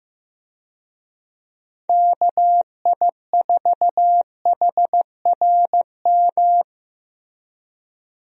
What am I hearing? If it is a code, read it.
Morse code, heard as a keyed tone, about 15 wpm, 700 Hz. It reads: KI4HRM
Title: USA Call Signs - 6 Characters - Repeated 3x - 15wpm